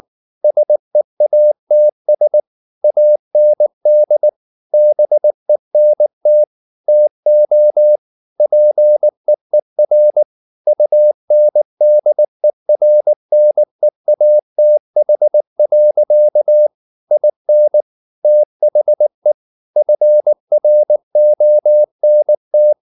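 Morse code, heard as a keyed tone, about 19 wpm, 600 Hz. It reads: SEATS AND BENT TO PEER UNDERNEATH. IN THE FRONT
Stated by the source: Morse code at 19 wpm from CW med Gnister, reading Ken Follett's novel Never in English.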